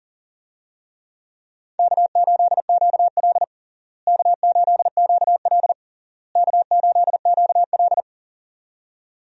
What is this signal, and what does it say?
Morse code, heard as a keyed tone, about 40 wpm, 700 Hz. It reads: K8QL K8QL K8QL
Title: USA Call Signs - 4 Characters - Repeated 3x - 40wpm